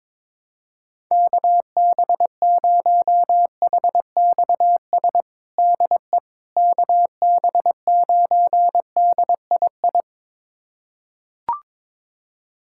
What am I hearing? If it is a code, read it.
Morse code, heard as a keyed tone, about 22 wpm, 700 Hz. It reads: KB0HXS DE KB9DII
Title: USA Callsign DE Callsign - 6 Characters - 22wpm